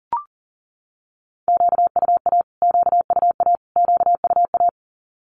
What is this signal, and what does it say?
Morse code, heard as a keyed tone, about 40 wpm, 700 Hz. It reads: QUA QUA QUA
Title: Q-Codes Repeated 3x 40wpm